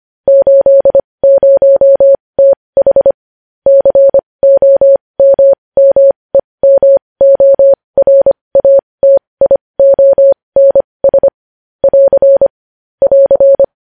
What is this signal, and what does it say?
Morse code, heard as a keyed tone, about 25 wpm, spaced as for 20 wpm, 550 Hz. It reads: 80TH COMMEMORATIONS <AR> <AR>
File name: podcast / News headlines-1x-25x20